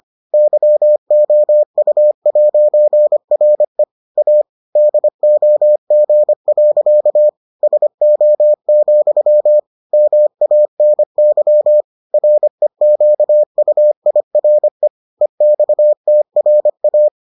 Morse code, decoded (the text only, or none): YOU'RE A DOG. SO, MANY REQUIRE EXTRA